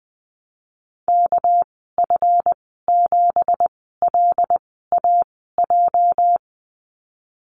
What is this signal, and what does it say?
Morse code, heard as a keyed tone, about 19 wpm, spaced as for 14 wpm, 700 Hz. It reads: KF7LAJ
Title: 213 Lesson 34g - Number 8 - Call Signs 20wpm